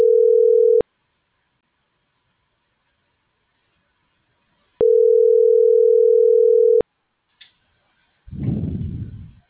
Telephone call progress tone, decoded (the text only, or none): ringback tone